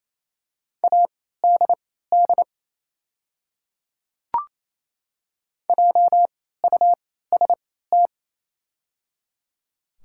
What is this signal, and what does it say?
Morse code, heard as a keyed tone, about 28 wpm, spaced as for 16 wpm, 700 Hz. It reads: ADD JUST
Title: 3x Character Spacing - Top 500 Words - 28wpm